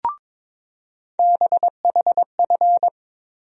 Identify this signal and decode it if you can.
Morse code, heard as a keyed tone, about 22 wpm, 700 Hz. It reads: BHF
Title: USA Call Sign Postfixes - 3 Letters 22wpm